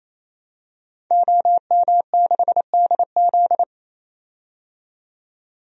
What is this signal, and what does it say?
Morse code, heard as a keyed tone, about 28 wpm, 700 Hz. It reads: OM6DZ
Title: DXCC-100 Call Signs - 5 Characters - 28wpm